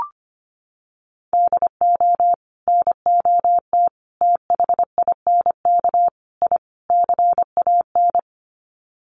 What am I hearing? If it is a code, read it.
Morse code, heard as a keyed tone, about 25 wpm, 700 Hz. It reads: DO NOT THINK I CAN
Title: Sets of 5 Words Encore2 25wpm